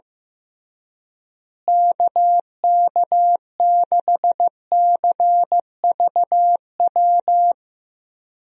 Morse code, heard as a keyed tone, about 15 wpm, 700 Hz. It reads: KK6CVW
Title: USA Call Signs - 6 Characters - Repeated 3x - 15wpm